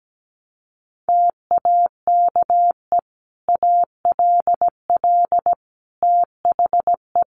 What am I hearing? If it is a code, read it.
Morse code, heard as a keyed tone, about 17 wpm, 700 Hz. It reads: TAKE ALL THE
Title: Sentences from Top 900 Words 17wpm